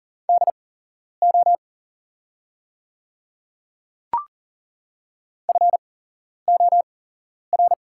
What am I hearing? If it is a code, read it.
Morse code, heard as a keyed tone, about 40 wpm, spaced as for 11 wpm, 690 Hz. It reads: DO FOR
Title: 8x Character Spacing - Top 100 Words - 40wpm